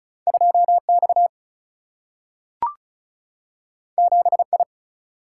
Morse code, heard as a keyed tone, about 36 wpm, 700 Hz. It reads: 2 X 7 I
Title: Single-Letter Single-Number 35wpm